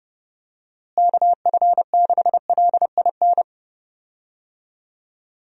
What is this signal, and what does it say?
Morse code, heard as a keyed tone, about 30 wpm, 700 Hz. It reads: KF6LIN
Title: USA Call Signs - 6 Characters - Repeated 2x - 30wpm